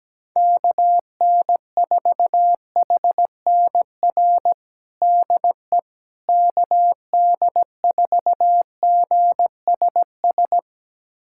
Morse code, heard as a keyed tone, about 17 wpm, 700 Hz. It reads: KN4HNR DE KD4GSS
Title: USA Callsign DE Callsign - 6 Characters - 17wpm